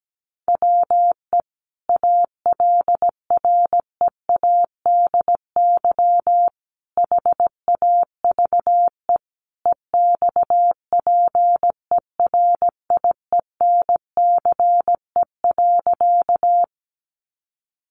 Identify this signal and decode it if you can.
Morse code, heard as a keyed tone, about 17 wpm, 700 Hz. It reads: WE ALREADY HAVE EXPERIENCE.
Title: Sentences from Top 700 Words 17wpm